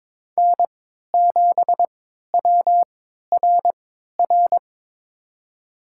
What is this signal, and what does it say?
Morse code, heard as a keyed tone, about 22 wpm, spaced as for 12 wpm, 700 Hz. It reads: N7WRR
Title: USA Call Signs - 5 Characters - 3x Character Spacing - 22wpm